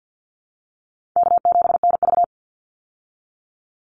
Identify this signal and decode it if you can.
Morse code, heard as a keyed tone, about 50 wpm, 700 Hz. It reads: K7NV
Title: Easy Call Signs Repeated 2x - 50wpm